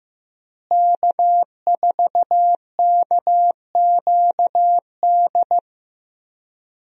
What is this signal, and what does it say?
Morse code, heard as a keyed tone, about 15 wpm, 700 Hz. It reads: K4KQD
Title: USA Call Signs - 5 Characters - Repeated 3x - 15wpm